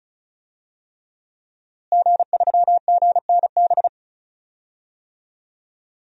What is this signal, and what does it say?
Morse code, heard as a keyed tone, about 34 wpm, 700 Hz. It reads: G3GNB
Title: DXCC-100 Call Signs - 5 Characters - Repeated 3x - 35wpm